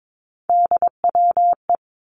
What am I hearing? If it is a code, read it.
Morse code, heard as a keyed tone, about 22 wpm, 700 Hz. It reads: DWE
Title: Three Challenging Letter-Number Combinations 22wpm